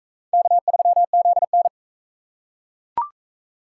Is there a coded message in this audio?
Yes: Morse code from 0.3 to 1.7 s, heard as a keyed tone, about 41 wpm, 700 Hz. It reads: K3ZN